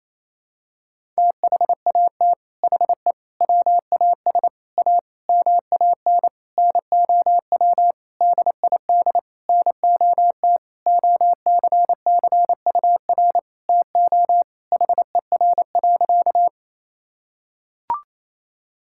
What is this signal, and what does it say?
Morse code, heard as a keyed tone, about 28 wpm, 700 Hz. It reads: THAT HE WAS A MAN NOW DID NOT OCCUR TO HER.
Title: Sentences from Top 1000 Words - Review Part 6 - 28wpm